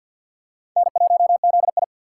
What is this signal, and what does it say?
Morse code, heard as a keyed tone, about 52 wpm, 700 Hz. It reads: N1ZI
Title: USA Call Signs - 4 Characters - 3x Character Spacing - 50wpm